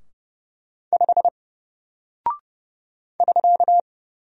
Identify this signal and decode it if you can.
Morse code, heard as a keyed tone, about 30 wpm, 700 Hz. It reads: 5 <SK>
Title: Single Character (Letter:Number:Punctuation:Prosign) Alt 30wpm